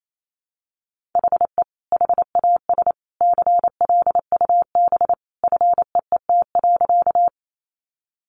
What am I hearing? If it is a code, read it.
Morse code, heard as a keyed tone, about 28 wpm, 700 Hz. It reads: HE HAS CLUB FEET.
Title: Sentences from Top 1200 Words 28wpm